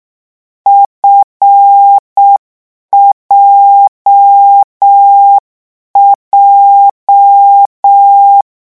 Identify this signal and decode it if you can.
Morse code, heard as a keyed tone, about 6 wpm, 800 Hz. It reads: FJJ